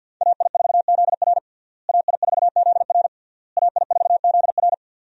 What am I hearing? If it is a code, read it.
Morse code, heard as a keyed tone, about 52 wpm, 700 Hz. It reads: AI4ZR AI4ZR AI4ZR
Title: USA Call Signs - 5 Characters - Repeated 3x - 50wpm